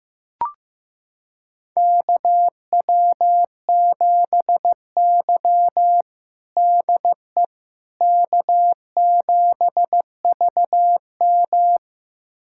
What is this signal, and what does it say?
Morse code, heard as a keyed tone, about 15 wpm, 700 Hz. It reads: KW7Y DE K7VM
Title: USA Callsign DE Callsign - 4 Characters - 15wpm